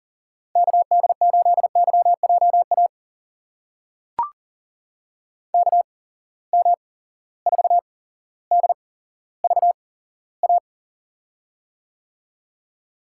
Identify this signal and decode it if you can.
Morse code, heard as a keyed tone, about 40 wpm, spaced as for 17 wpm, 700 Hz. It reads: KD8YJA KM4DVA
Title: USA Call Signs - 6 Characters - 8x Character Spacing - 40wpm